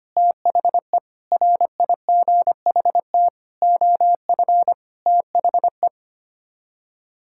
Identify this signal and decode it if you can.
Morse code, heard as a keyed tone, about 25 wpm, 700 Hz. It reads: THE RIGHT OF THE
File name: Sets of 9 Words - 25wpm